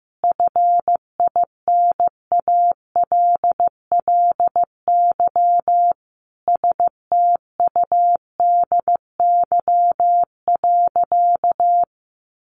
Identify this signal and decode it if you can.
Morse code, heard as a keyed tone, about 15 wpm, 700 Hz. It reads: FINALLY STUDY.
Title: Additional Easy Phrases 15wpm